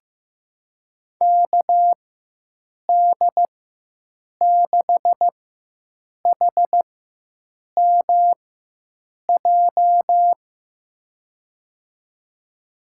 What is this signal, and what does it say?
Morse code, heard as a keyed tone, about 15 wpm, spaced as for 7 wpm, 700 Hz. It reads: KD6HMJ
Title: USA Call Signs - 6 Characters - 4x Character Spacing - 15wpm